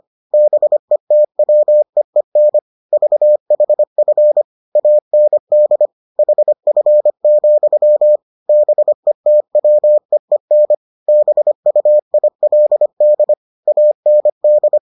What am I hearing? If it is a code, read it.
Morse code, heard as a keyed tone, about 25 wpm, 600 Hz. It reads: BETWEEN VHF AND HF, BETWEEN BUILD AND